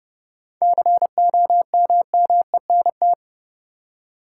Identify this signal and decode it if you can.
Morse code, heard as a keyed tone, about 30 wpm, 700 Hz. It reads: COMMENT
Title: Seven Letter Words - 30wpm